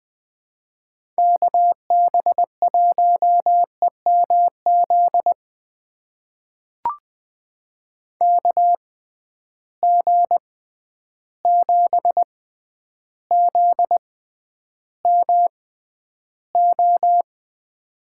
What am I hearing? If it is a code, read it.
Morse code, heard as a keyed tone, about 20 wpm, spaced as for 10 wpm, 700 Hz. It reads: KB1EMZ KG7ZMO
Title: USA Call Signs - 6 Characters - 6x Character Spacing - 20wpm